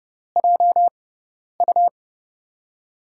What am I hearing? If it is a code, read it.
Morse code, heard as a keyed tone, about 32 wpm, spaced as for 10 wpm, 700 Hz. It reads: JU
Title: USA Call Signs - 6 Characters - 6x Character Spacing - 30wpm